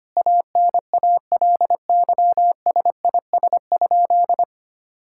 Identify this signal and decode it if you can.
Morse code, heard as a keyed tone, about 25 wpm, 700 Hz. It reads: ANALYSIS?
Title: Sentences from Top 1000 Words - Review Part 5 - 25wpm